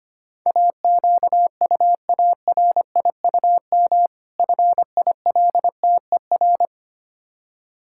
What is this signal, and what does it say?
Morse code, heard as a keyed tone, about 25 wpm, 700 Hz. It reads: AQUARIUM FILTER
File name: practice / 25wpm / Difficult Words with Hints - 25wpm